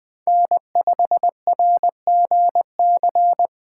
Morse code, heard as a keyed tone, about 20 wpm, 700 Hz. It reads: N5RGC
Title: USA Call Signs - 6x Character Spacing - 20wpm